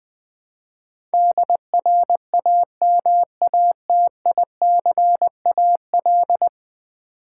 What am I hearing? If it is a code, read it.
Morse code, heard as a keyed tone, about 20 wpm, 700 Hz. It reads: DRAMATICAL
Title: Build A Word - 5 Letters At A Time - 12-Letter Words - 20wpm